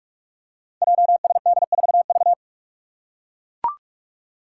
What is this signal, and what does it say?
Morse code, heard as a keyed tone, about 44 wpm, 700 Hz. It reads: JSD4V
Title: Five Challenging Letter-Number Combinations 45wpm